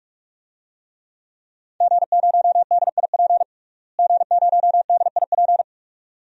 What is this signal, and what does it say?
Morse code, heard as a keyed tone, about 44 wpm, 700 Hz. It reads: G0DIP G0DIP
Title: DXCC-100 Call Signs - 5 Characters - Repeated 2x - 45wpm